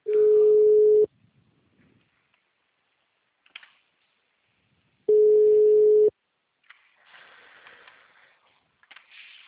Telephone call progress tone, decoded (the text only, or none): ringback tone